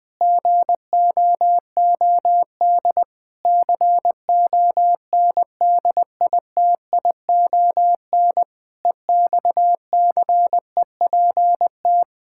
Morse code, heard as a keyed tone, about 21 wpm, 700 Hz. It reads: GOOD CONDITION EXCEPT
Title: Pangram Challenge v2 - 20wpm